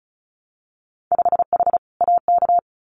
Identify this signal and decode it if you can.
Morse code, heard as a keyed tone, about 36 wpm, 700 Hz. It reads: 5H AK
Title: Winter Field Day Exchanges Encore 35wpm